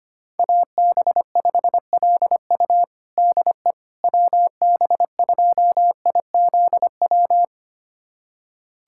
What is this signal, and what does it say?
Morse code, heard as a keyed tone, about 25 wpm, 700 Hz. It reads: AB5LU DE WB2IZW
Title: USA Callsign DE Callsign 25wpm